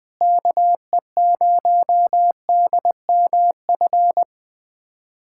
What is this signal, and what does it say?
Morse code, heard as a keyed tone, about 20 wpm, 700 Hz. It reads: KE0DMF
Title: USA Callsign DE Callsign 20wpm